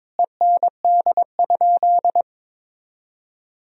Morse code, heard as a keyed tone, about 22 wpm, 700 Hz. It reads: END?
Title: Sentences from Top 1000 Words - Review Part 4 - 22wpm